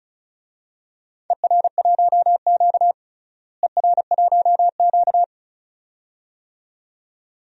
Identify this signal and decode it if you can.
Morse code, heard as a keyed tone, about 36 wpm, 700 Hz. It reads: ER1Q ER1Q
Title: DXCC-100 Call Signs - 4 Characters - Repeated 2x - 35wpm